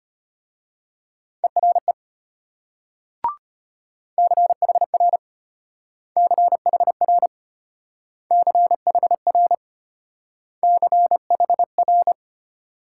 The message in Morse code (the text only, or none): ERE CHR CHR CHR CHR